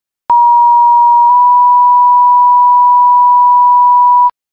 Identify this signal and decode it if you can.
Two-tone page, heard as a one-second tone, then a three-second tone, about 950 then 980 Hz. It is heard from 0.3 to 4.3 s.